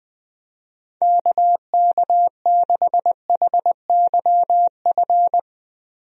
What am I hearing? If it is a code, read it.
Morse code, heard as a keyed tone, about 20 wpm, 700 Hz. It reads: KK6HYF
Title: USA Call Signs - 6 Characters - Repeated 3x - 20wpm